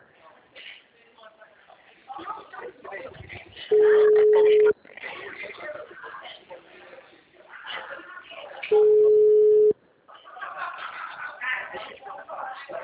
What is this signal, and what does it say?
Ringback tone (Europe, 425 Hz), with speech.